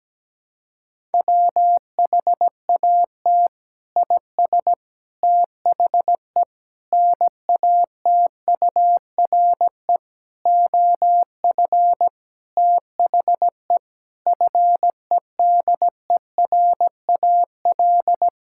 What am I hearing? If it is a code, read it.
Morse code, heard as a keyed tone, about 17 wpm, 700 Hz. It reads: WHAT IS THE NATURE OF THE FEDERAL
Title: Sentences from Top 1000 Words - Review Part 5 - 17wpm